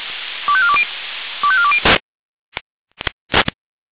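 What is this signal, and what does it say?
Five-tone selective call (ZVEI), heard as quick runs of five tones, tones of about 70 ms. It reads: 25420 25420
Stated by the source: ZVEI five-tone selective call received on an SDR.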